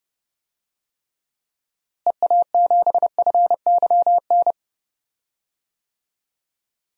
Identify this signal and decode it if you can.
Morse code, heard as a keyed tone, about 30 wpm, 700 Hz. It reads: EA7FYN
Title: DXCC-100 Call Signs - 6 Characters - Repeated 2x - 30wpm